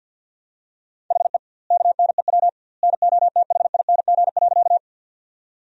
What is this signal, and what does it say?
Morse code, heard as a keyed tone, about 46 wpm, 700 Hz. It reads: HE KNEW NOTHING.
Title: 2x Word-Spacing - Sentences from Top 500 Words - 50wpm